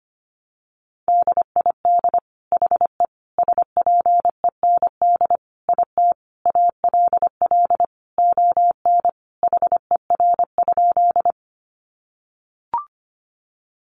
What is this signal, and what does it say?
Morse code, heard as a keyed tone, about 25 wpm, 700 Hz. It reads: DID HE SPEND IT ALL ON HER?
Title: Sentences from Top 800 Words 25wpm